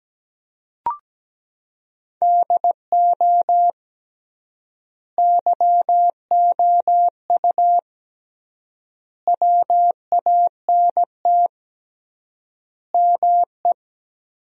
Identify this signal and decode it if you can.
Morse code, heard as a keyed tone, about 17 wpm, 700 Hz. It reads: DO YOU WANT ME
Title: 3x Word-Spacing - Sentences from Top 200 Words - 17wpm